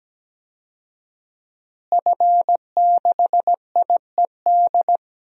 Morse code, heard as a keyed tone, about 17 wpm, 700 Hz. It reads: F6IED
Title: DXCC-100 Call Signs - 5 Characters - Repeated 2x - 17wpm